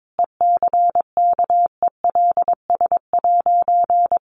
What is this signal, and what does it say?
Morse code, heard as a keyed tone, about 22 wpm, 700 Hz. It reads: ECKELS'
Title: A Sound Of Thunder 22wpm